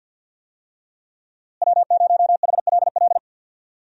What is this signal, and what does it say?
Morse code, heard as a keyed tone, about 52 wpm, 700 Hz. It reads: W0HLL
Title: USA Call Signs - 5 Characters - Repeated 2x - 50wpm